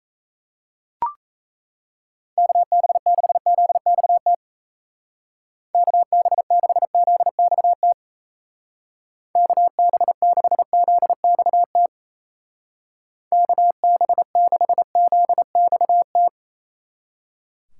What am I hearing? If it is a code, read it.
Morse code, heard as a keyed tone, about 35 wpm, 700 Hz. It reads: KB6ZXT KB6ZXT KB6ZXT KB6ZXT
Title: Speed Racing - USA 2x3 Call Signs 28wpm